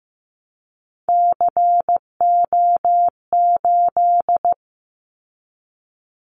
Morse code, heard as a keyed tone, about 15 wpm, 700 Hz. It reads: CO8
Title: Callsign Prefixes for Top 100 DXCC - 15wpm